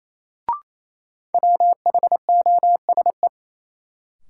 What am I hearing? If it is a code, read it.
Morse code, heard as a keyed tone, about 28 wpm, 700 Hz. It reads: WHOSE
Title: Sentences from Top 1200 Words - New Word Drill - 28wpm